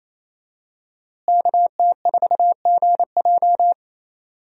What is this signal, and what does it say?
Morse code, heard as a keyed tone, about 27 wpm, 700 Hz. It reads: KT4GJ
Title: USA Call Signs - 5 Characters - Repeated 2x - 28wpm